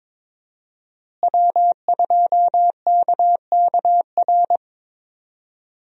Morse code, heard as a keyed tone, about 21 wpm, 700 Hz. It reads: W2KKR
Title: USA Call Signs - 5 Characters - Repeated 2x - 22wpm